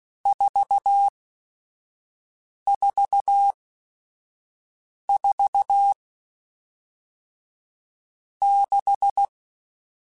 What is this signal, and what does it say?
Morse code, heard as a keyed tone, about 16 wpm, spaced as for 5 wpm, 800 Hz. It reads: 444 6
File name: Groups@5 / Groups@5 mp3s / VB46@5